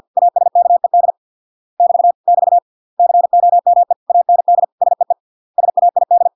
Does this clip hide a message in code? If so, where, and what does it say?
Yes: Morse code from 0.2 to 6.3 s, heard as a keyed tone, about 50 wpm, 700 Hz. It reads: ASKED <BT> <BT> COME AND SEE SAID